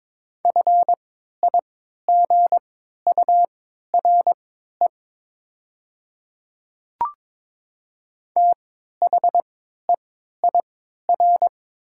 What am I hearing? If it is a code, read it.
Morse code, heard as a keyed tone, about 22 wpm, spaced as for 12 wpm, 700 Hz. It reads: FIGURE THEIR
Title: 3x Character Spacing - Top 500 Words - 22wpm